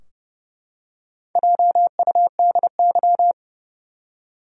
Morse code, heard as a keyed tone, about 30 wpm, 700 Hz. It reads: JUDY
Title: USA Common Names - Max 4 Letters - 30wpm